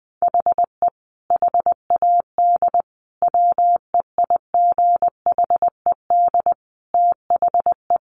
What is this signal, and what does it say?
Morse code, heard as a keyed tone, about 20 wpm, 700 Hz. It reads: HE HAD WEIGHED THE
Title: A Sound Of Thunder 20wpm